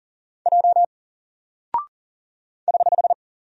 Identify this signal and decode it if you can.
Morse code, heard as a keyed tone, about 39 wpm, 700 Hz. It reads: J <HH>
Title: Single Character (Letter:Number:Punctuation:Prosign) Alt 40wpm